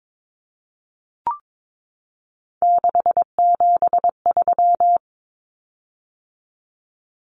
Morse code, heard as a keyed tone, about 22 wpm, 700 Hz. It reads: 673
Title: Common Cut Numbers - 3-Digit 22wpm